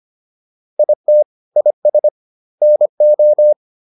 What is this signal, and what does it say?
Morse code, heard as a keyed tone, about 25 wpm, 600 Hz. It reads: IT IS NO